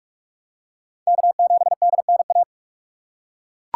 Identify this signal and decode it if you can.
Morse code, heard as a keyed tone, about 41 wpm, 690 Hz. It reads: K7DNA
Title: USA Call Signs - 5 Characters - 2x Character Spacing - 45wpm